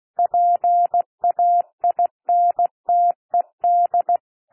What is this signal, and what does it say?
Morse code, heard as a keyed tone, about 16 wpm, 700 Hz. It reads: PAINTED
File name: cwmp3 / P1 SOUND PASSAGE 16 wpm